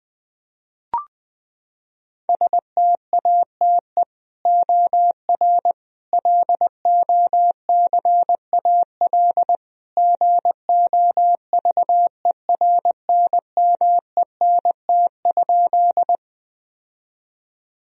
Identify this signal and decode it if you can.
Morse code, heard as a keyed tone, about 20 wpm, 700 Hz. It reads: STATE OR LOCAL GOVERNMENT?
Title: Sentences from Top 1000 Words - Review Part 3 - 20wpm